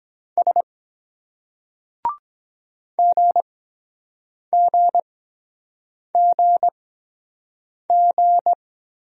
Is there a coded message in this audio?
Yes: Morse code from 0.4 to 8.5 s, heard as a keyed tone, about 22 wpm, spaced as for 7 wpm, 700 Hz. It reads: S GGGG